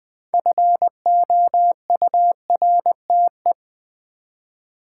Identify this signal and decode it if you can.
Morse code, heard as a keyed tone, about 20 wpm, 700 Hz. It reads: FOURTE
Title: Build A Word - 3 Letters At A Time - 8-Letter Words - 20wpm